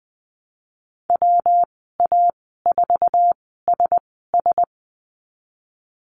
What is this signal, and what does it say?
Morse code, heard as a keyed tone, about 20 wpm, spaced as for 15 wpm, 700 Hz. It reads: WA4SS
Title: 147 Lesson 24g - Letter y - Call Signs 20wpm